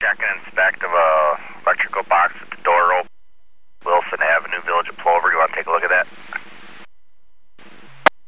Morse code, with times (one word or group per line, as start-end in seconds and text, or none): none